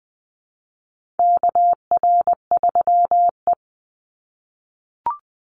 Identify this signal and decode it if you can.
Morse code, heard as a keyed tone, about 20 wpm, 700 Hz. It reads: KR3E
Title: Call Signs - 20wpm